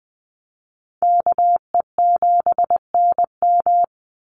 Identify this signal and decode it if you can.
Morse code, heard as a keyed tone, about 20 wpm, 700 Hz. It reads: KE7NM
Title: USA Call Signs 20wpm